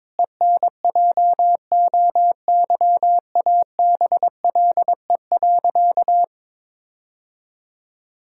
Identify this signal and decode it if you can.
Morse code, heard as a keyed tone, about 22 wpm, 700 Hz. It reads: ENJOYABLE.
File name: Difficult Sentences with Hints - 22wpm